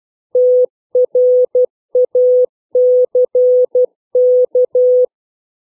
Morse code, heard as a keyed tone, about 12 wpm, 500 Hz. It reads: TRACK
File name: CW / fixed length words